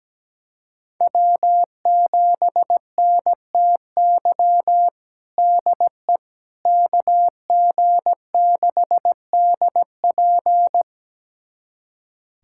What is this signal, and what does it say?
Morse code, heard as a keyed tone, about 17 wpm, 700 Hz. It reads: W7NTY DE KG6DP
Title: USA Callsign DE Callsign - 5 Characters - 17wpm